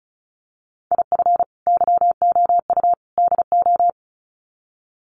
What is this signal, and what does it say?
Morse code, heard as a keyed tone, about 34 wpm, 700 Hz. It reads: IF YOU DO
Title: Sets of 3 Words 35wpm